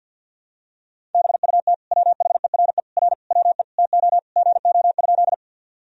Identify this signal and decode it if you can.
Morse code, heard as a keyed tone, about 50 wpm, 700 Hz. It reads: BUT WHERE R WE TO GO?
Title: Sentences from Top 200 Words and Abbreviations 50wpm